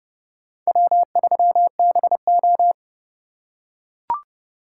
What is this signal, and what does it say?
Morse code, heard as a keyed tone, about 27 wpm, 700 Hz. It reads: W3BO